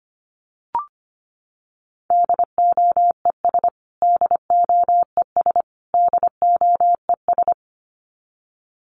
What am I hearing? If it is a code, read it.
Morse code, heard as a keyed tone, about 25 wpm, 700 Hz. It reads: DOES DOES DOES
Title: Top 500 Words Repeated 25wpm